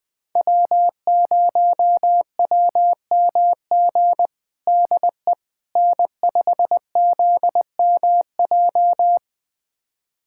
Morse code, heard as a keyed tone, about 21 wpm, 700 Hz. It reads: W0WMG DE N5ZMJ
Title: USA Callsign DE Callsign - 5 Characters - 20wpm